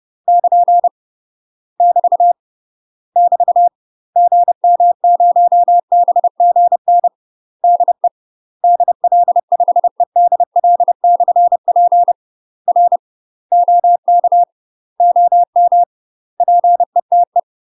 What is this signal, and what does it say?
Morse code, heard as a keyed tone, about 30 wpm, 700 Hz. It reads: <KN> <BT> <BT> GM0BGN DE DL5EDL/P R OK OM PETE
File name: UK MASTER QSOs TRAINING / 30WPM / UK MASTER QSOs 0013